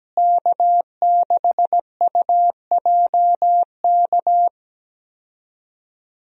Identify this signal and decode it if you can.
Morse code, heard as a keyed tone, about 17 wpm, 700 Hz. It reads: K6UJK
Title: USA Call Signs - 5 Characters - Repeated 2x - 17wpm